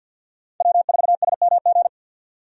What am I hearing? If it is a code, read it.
Morse code, heard as a keyed tone, about 52 wpm, 700 Hz. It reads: W4SMG
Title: USA Call Signs - 5 Characters - 50wpm